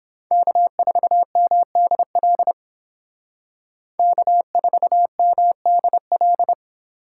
Morse code, heard as a keyed tone, about 28 wpm, 700 Hz. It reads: K4MDL K4MDL